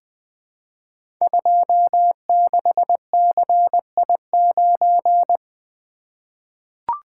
Morse code, heard as a keyed tone, about 20 wpm, 700 Hz. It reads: 26CI9
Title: Code Groups - 5 Letter:Numbers - 20wpm